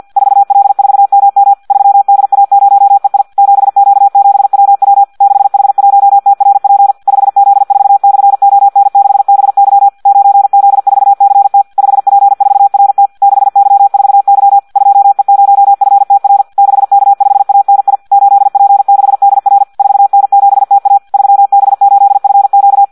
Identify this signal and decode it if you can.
Morse code, heard as a keyed tone, about 50 wpm, 800 Hz. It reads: 4L4MM 3DA0EI 7Q7WW 6H1TRP 5Z4/ON6BY 9Z4CT 5P4RT 6Y4Y 2E0FTR 6K5ANI 8P6DR 4N7TA 3B8FQ
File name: Audio practica CW / qrq-tx-50 2